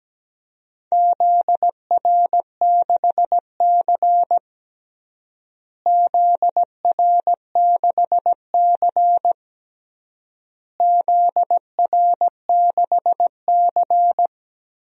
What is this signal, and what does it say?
Morse code, heard as a keyed tone, about 17 wpm, 700 Hz. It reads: ZR6C ZR6C ZR6C
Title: DXCC-100 Call Signs - 4 Characters - Repeated 3x - 17wpm